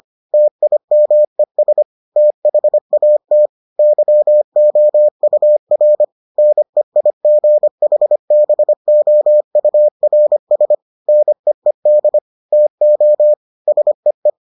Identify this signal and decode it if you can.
Morse code, heard as a keyed tone, about 25 wpm, 600 Hz. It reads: TIMES THAT YOUR NEIGHBOURS NEED TO SEE